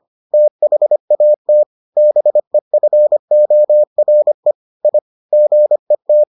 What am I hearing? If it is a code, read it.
Morse code, heard as a keyed tone, about 25 wpm, 600 Hz. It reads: THAT BEFORE I GET